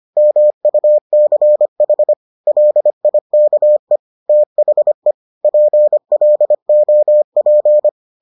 Morse code, heard as a keyed tone, about 25 wpm, 600 Hz. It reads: MUCH LIKE THE PLOP